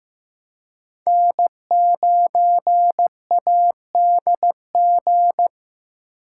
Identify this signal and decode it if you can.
Morse code, heard as a keyed tone, about 15 wpm, 700 Hz. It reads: N9ADG
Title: K1USN SST QSO Practice - 15wpm